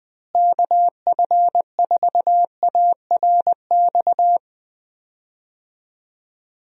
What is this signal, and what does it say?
Morse code, heard as a keyed tone, about 20 wpm, 700 Hz. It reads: KF4ARX
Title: USA Call Signs - 6 Characters - Repeated 2x - 20wpm